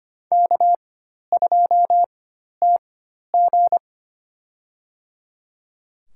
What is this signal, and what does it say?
Morse code, heard as a keyed tone, about 25 wpm, spaced as for 12 wpm, 700 Hz. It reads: K2TG
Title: USA Call Signs - 4x Character Spacing - 25wpm